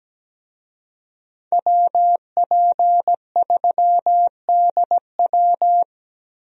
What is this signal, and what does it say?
Morse code, heard as a keyed tone, about 17 wpm, 700 Hz. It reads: WP3DW
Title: USA Call Signs - 5 Characters - Repeated 2x - 17wpm